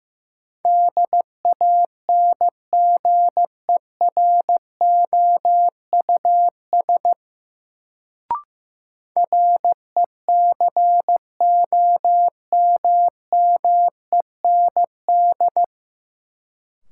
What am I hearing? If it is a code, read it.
Morse code, heard as a keyed tone, about 15 wpm, 700 Hz. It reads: DANGEROUS RECOMMEND
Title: Sentences from Top 1200 Words - New Word Drill Alt - 15wpm